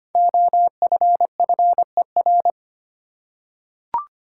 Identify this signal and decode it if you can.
Morse code, heard as a keyed tone, about 25 wpm, 700 Hz. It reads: OFFER
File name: 5 Letter Words - First 3 Letters Given - 25wpm